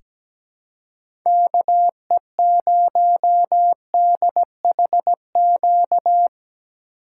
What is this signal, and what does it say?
Morse code, heard as a keyed tone, about 17 wpm, 700 Hz. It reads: KE0DHQ
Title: USA Call Signs - 6 Characters - 3x Character Spacing - 17wpm